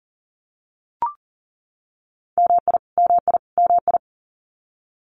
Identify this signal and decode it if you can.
Morse code, heard as a keyed tone, about 40 wpm, 700 Hz. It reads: MI MI MI
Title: US States Repeated 3x 40wpm